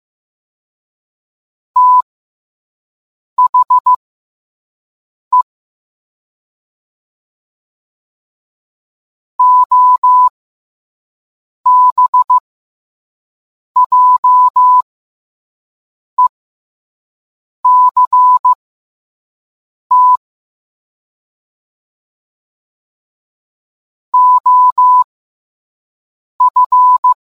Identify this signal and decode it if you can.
Morse code, heard as a keyed tone, about 15 wpm, spaced as for 5 wpm, 1000 Hz. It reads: THE OBJECT OF